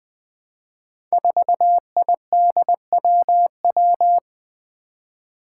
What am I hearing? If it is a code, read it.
Morse code, heard as a keyed tone, about 20 wpm, 700 Hz. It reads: 4IDWW
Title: Five Challenging Letter-Number Combinations 20wpm